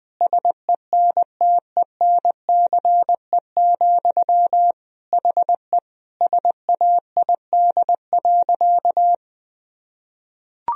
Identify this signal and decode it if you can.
Morse code, heard as a keyed tone, about 20 wpm, 700 Hz. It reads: SENTENCE, HE SAID.
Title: Sentences from Top 1000 Words - Review Part 1 - 20wpm